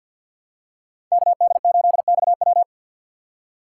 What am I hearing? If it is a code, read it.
Morse code, heard as a keyed tone, about 46 wpm, 700 Hz. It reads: KD8XW
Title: USA Call Signs - 5 Characters - 4x Character Spacing - 50wpm